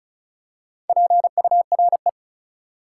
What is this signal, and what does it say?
Morse code, heard as a keyed tone, about 34 wpm, 700 Hz. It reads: PURE
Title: Four Letter Words - 35wpm